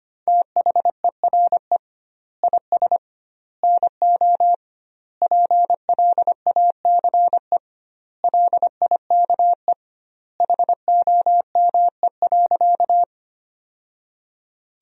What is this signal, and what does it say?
Morse code, heard as a keyed tone, about 25 wpm, 700 Hz. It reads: THERE IS NO PLACE LIKE HOME.
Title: 2x Word-Spacing - Sentences from Top 200 Words - 25wpm